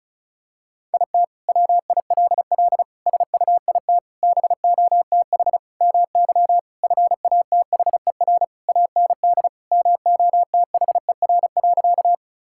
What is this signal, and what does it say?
Morse code, heard as a keyed tone, about 36 wpm, 700 Hz. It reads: IT WILL SUIT BOTH MY FATHER AND MOTHER.
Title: Sentences from Top 1600 Words 35wpm